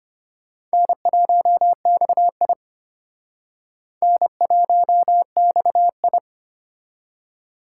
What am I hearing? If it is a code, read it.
Morse code, heard as a keyed tone, about 32 wpm, 700 Hz. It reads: N1XI N1XI